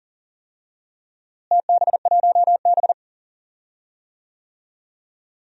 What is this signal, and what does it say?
Morse code, heard as a keyed tone, about 39 wpm, 700 Hz. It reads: TB1B
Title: DXCC-100 Call Signs - 4 Characters - Repeated 2x - 40wpm